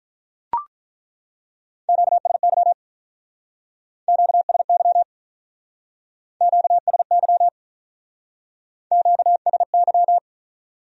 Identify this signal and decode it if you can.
Morse code, heard as a keyed tone, about 47 wpm, 700 Hz. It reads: QSY QSY QSY QSY